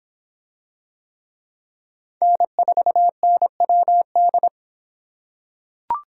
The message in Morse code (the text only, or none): N4NWD